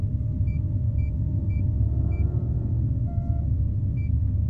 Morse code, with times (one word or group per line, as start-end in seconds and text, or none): none